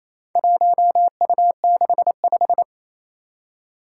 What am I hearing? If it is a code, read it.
Morse code, heard as a keyed tone, about 27 wpm, 700 Hz. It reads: 1U65
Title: Four Challenging Letter-Number Combinations 28wpm